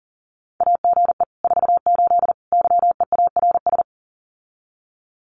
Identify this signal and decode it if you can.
Morse code, heard as a keyed tone, about 40 wpm, 700 Hz. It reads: AGE 48 YEARS
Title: Common QSO Phrases Encore 40wpm